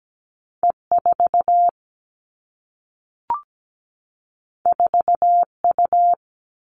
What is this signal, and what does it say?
Morse code, heard as a keyed tone, about 17 wpm, 700 Hz. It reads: E 4 4 U
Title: Two Challenging Letter-Number Combinations 17wpm